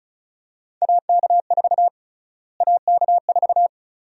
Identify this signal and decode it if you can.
Morse code, heard as a keyed tone, about 36 wpm, 700 Hz. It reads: AK4 AK4
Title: USA Call Sign Prefixes - Repeated 2x - 35wpm